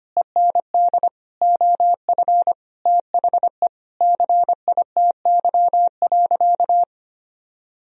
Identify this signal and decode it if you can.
Morse code, heard as a keyed tone, about 25 wpm, 700 Hz. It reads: END OF THE CITY.
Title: Sentences from Top 1000 Words - Review Part 2 - 25wpm